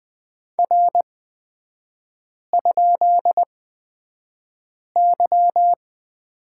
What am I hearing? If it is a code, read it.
Morse code, heard as a keyed tone, about 20 wpm, spaced as for 5 wpm, 700 Hz. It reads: R?Y
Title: 250 Lesson 40e - BK - Three Characters 20wpm